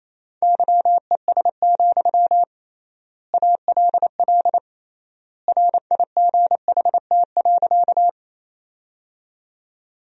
Morse code, heard as a keyed tone, about 28 wpm, 700 Hz. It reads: YES, ALL RIGHT.
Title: 3x Word-Spacing - Sentences from Top 500 Words - 28wpm